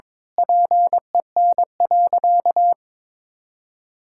Morse code, heard as a keyed tone, about 22 wpm, 700 Hz. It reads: PEN.
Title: Pangram Challenge v2 - 22wpm